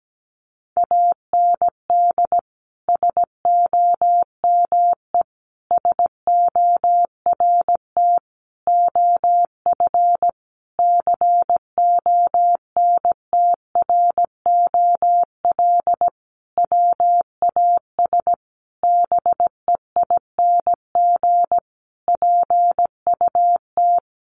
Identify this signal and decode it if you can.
Morse code, heard as a keyed tone, about 17 wpm, 700 Hz. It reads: AND SOME SORT OF CONTROL WAS BEING PUT
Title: Sentences from Top 1300 Words 17wpm